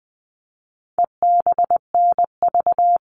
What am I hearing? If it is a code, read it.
Morse code, heard as a keyed tone, about 20 wpm, 700 Hz. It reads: EBNV
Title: All Cut Numbers - 4-Digit 20wpm